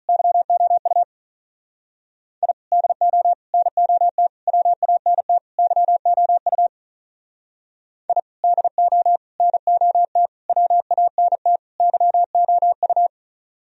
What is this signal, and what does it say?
Morse code, heard as a keyed tone, about 46 wpm, 700 Hz. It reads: YOU I DO NOT WANT YOU I DO NOT WANT YOU